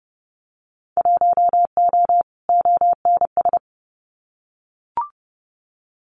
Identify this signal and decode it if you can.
Morse code, heard as a keyed tone, about 32 wpm, 700 Hz. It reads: 1O ONS
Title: Winter Field Day Exchanges Encore 30wpm